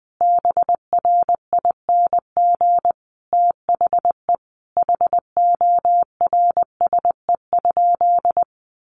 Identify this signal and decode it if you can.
Morse code, heard as a keyed tone, about 20 wpm, 700 Hz. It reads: BRING THE HORSE?
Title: Sentences from Top 500 Words 20wpm